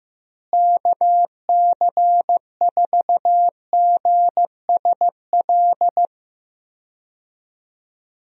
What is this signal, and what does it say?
Morse code, heard as a keyed tone, about 15 wpm, 700 Hz. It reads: KC4GSL
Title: USA Call Signs - 6 Characters - Repeated 2x - 15wpm